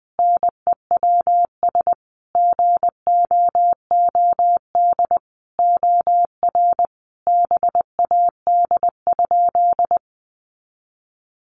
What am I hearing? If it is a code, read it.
Morse code, heard as a keyed tone, about 19 wpm, 700 Hz. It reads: NEWS GOOD OR BAD?
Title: Sentences from Top 800 Words 20wpm